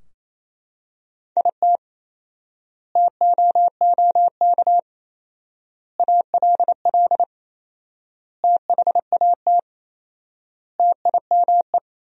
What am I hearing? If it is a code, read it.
Morse code, heard as a keyed tone, about 28 wpm, 700 Hz. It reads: IT TOOK ALL THAT TIME
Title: 4x Word-Spacing - Sentences from Top 300 Words - 28wpm